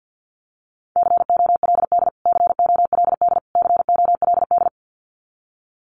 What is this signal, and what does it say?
Morse code, heard as a keyed tone, about 50 wpm, 700 Hz. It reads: COLD COLD COLD
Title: Top 500 Words Repeated 50wpm